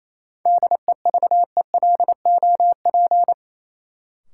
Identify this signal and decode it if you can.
Morse code, heard as a keyed tone, about 28 wpm, 700 Hz. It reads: DEVELOP
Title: Sentences from Top 1000 Words - Review Part 2 - New Word Drill - 28wpm